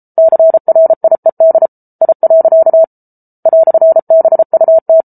Morse code, heard as a keyed tone, about 34 wpm, 650 Hz. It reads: CRIED I. "BUT